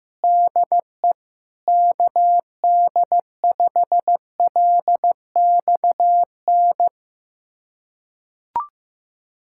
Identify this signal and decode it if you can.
Morse code, heard as a keyed tone, about 15 wpm, 700 Hz. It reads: DE KD5LXN
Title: USA Callsign DE Callsign 15wpm